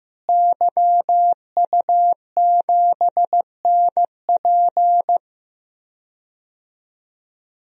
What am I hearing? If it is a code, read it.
Morse code, heard as a keyed tone, about 15 wpm, 700 Hz. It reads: YU7NP
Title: DXCC-100 Call Signs - 5 Characters - Repeated 2x - 15wpm